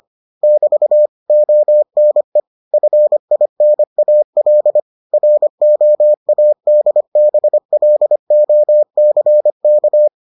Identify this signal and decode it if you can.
Morse code, heard as a keyed tone, about 25 wpm, 600 Hz. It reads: <BT> ONE FINAL ROADBLOCK